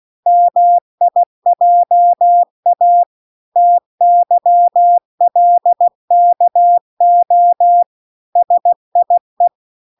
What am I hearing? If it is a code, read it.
Morse code, heard as a keyed tone, about 16 wpm, 700 Hz. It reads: MIJA TYLKO SIE